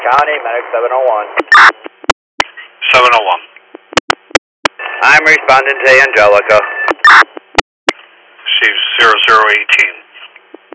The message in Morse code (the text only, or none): none